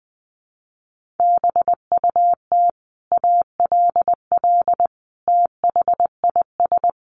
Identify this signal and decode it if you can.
Morse code, heard as a keyed tone, about 19 wpm, 700 Hz. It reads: BUT ALL THIS
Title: Sentences from Top 1500 Words 20wpm